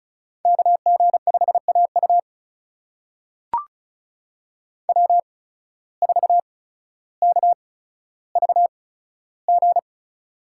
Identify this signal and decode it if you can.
Morse code, heard as a keyed tone, about 36 wpm, spaced as for 15 wpm, 700 Hz. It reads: KG5AU W4KVG
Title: USA Call Signs - 5 Characters - 8x Character Spacing - 35wpm